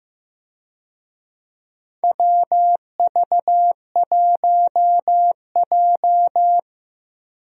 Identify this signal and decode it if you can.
Morse code, heard as a keyed tone, about 15 wpm, 700 Hz. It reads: WV1J
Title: USA Call Signs - 4 Characters - Repeated 2x - 15wpm